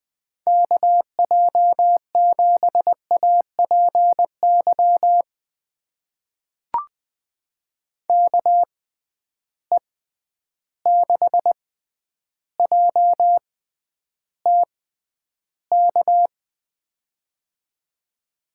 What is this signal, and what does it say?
Morse code, heard as a keyed tone, about 20 wpm, spaced as for 10 wpm, 700 Hz. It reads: KJ7APY KE6JTK
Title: USA Call Signs - 6 Characters - 6x Character Spacing - 20wpm